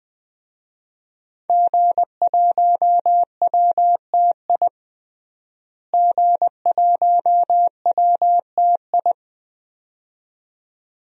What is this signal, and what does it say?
Morse code, heard as a keyed tone, about 20 wpm, 700 Hz. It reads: G1WTI G1WTI
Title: DXCC-100 Call Signs - 5 Characters - Repeated 2x - 20wpm